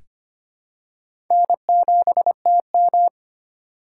Morse code, heard as a keyed tone, about 25 wpm, 700 Hz. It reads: N7TM
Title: USA Call Signs - 2x Character Spacing - 25wpm